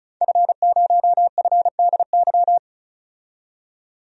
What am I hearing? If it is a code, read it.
Morse code, heard as a keyed tone, about 34 wpm, 700 Hz. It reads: F0FDY
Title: DXCC-100 Call Signs - 5 Characters - 35wpm